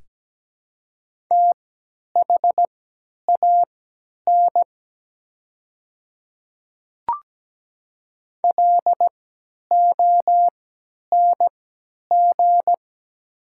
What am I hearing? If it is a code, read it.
Morse code, heard as a keyed tone, about 17 wpm, spaced as for 10 wpm, 700 Hz. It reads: THAN LONG
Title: 3x Character Spacing - Top 100 Words - 17wpm